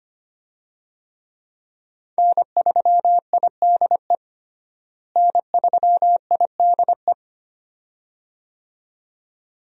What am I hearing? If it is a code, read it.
Morse code, heard as a keyed tone, about 25 wpm, 700 Hz. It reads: N3IDE N3IDE
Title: USA Call Signs - 5 Characters - Repeated 2x - 25wpm